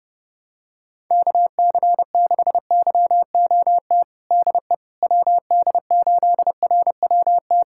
Morse code, heard as a keyed tone, about 30 wpm, 700 Hz. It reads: KC6YOT DE WD8RWT
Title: USA Callsign DE Callsign 30wpm